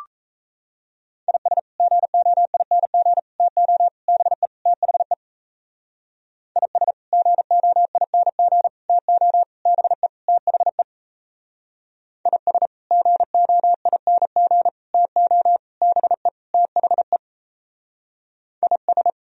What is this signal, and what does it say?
Morse code, heard as a keyed tone, about 35 wpm, 700 Hz. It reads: IS GOING TO BE THE IS GOING TO BE THE IS GOING TO BE THE IS